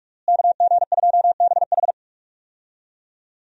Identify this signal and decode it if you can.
Morse code, heard as a keyed tone, about 48 wpm, 700 Hz. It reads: KG2BH
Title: USA Call Signs - 5 Characters - Repeated 3x - 45wpm